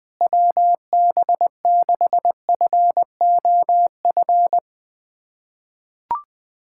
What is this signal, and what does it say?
Morse code, heard as a keyed tone, about 20 wpm, 700 Hz. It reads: WB6FOF
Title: USA Callsign DE Callsign 20wpm